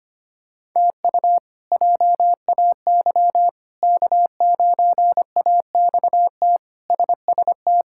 Morse code, heard as a keyed tone, about 25 wpm, 700 Hz. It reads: TU JAY K9AXT SST
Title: K1USN SST QSO Practice - 25wpm